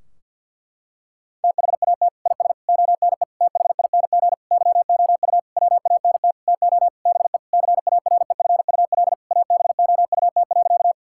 Morse code, heard as a keyed tone, about 46 wpm, 700 Hz. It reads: THAT IS ONE THING YOU WANT TO BE CAREFUL ABOUT.
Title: Sentences from Top 1600 Words 50wpm